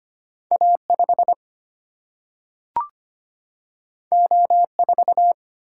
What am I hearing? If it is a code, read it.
Morse code, heard as a keyed tone, about 25 wpm, 700 Hz. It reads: A 5 O 4
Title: Single-Letter Single-Number 25wpm